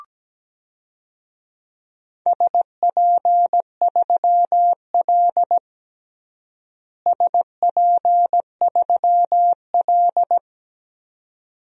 Morse code, heard as a keyed tone, about 17 wpm, 700 Hz. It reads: SP3L SP3L
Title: DXCC-100 Call Signs - 4 Characters - Repeated 2x - 17wpm